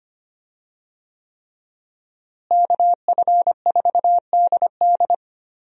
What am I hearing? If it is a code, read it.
Morse code, heard as a keyed tone, about 25 wpm, 700 Hz. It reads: KF4DD
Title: USA Call Signs - 5 Characters - Repeated 3x - 25wpm